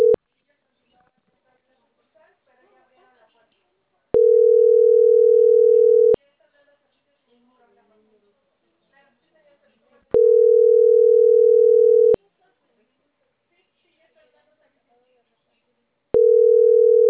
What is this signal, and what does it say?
Ringback tone (North America), 440 + 480 Hz, with speech.
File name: CENTRAL CAEA / out agcaea01